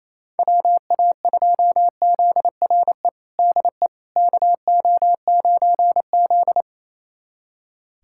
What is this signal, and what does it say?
Morse code, heard as a keyed tone, about 28 wpm, 700 Hz. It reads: WA2ZRE DE KO9Z
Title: USA Callsign DE Callsign 28wpm